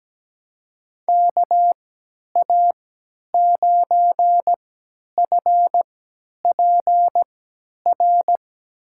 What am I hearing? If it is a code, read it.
Morse code, heard as a keyed tone, about 17 wpm, spaced as for 10 wpm, 700 Hz. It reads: KA9FPR
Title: USA Call Signs - 6 Characters - 3x Character Spacing - 17wpm